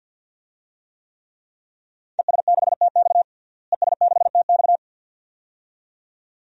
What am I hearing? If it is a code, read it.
Morse code, heard as a keyed tone, about 51 wpm, 700 Hz. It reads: ES6TX ES6TX
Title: DXCC-100 Call Signs - 5 Characters - Repeated 2x - 50wpm